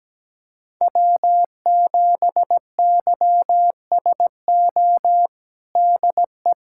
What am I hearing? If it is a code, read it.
Morse code, heard as a keyed tone, about 17 wpm, 700 Hz. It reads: W7YSO DE
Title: USA Callsign DE Callsign - 5 Characters - 17wpm